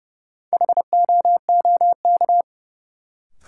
Morse code, heard as a keyed tone, about 30 wpm, 700 Hz. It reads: HOOK
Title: Sentences from Top 1600 Words - New Word Drill - 30wpm